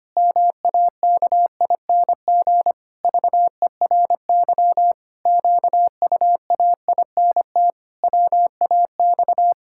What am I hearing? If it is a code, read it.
Morse code, heard as a keyed tone, about 25 wpm, 700 Hz. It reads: MAKING VERY QUAINT WAX
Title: Pangram Challenge v2 - 25wpm